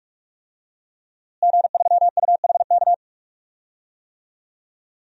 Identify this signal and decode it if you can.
Morse code, heard as a keyed tone, about 46 wpm, 700 Hz. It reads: G3UHK
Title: DXCC-100 Call Signs - 5 Characters - Repeated 3x - 45wpm